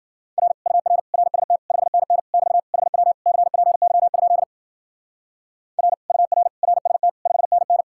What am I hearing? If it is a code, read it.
Morse code, heard as a keyed tone, about 51 wpm, 700 Hz. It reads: R UR RST 5NN <BT> HW CPY? R UR RST 5NN